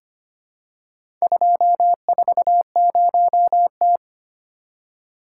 Morse code, heard as a keyed tone, about 25 wpm, 700 Hz. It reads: 240T
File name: practice / 25wpm / Four Letter-Number Combinations - 25wpm